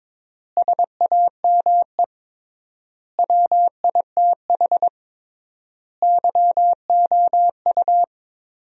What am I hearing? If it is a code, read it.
Morse code, heard as a keyed tone, about 22 wpm, 700 Hz. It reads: SAME WITH YOU
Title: 3x Word-Spacing - Sentences from Top 300 Words - 22wpm